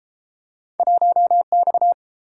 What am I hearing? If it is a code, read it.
Morse code, heard as a keyed tone, about 30 wpm, 700 Hz. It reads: 1X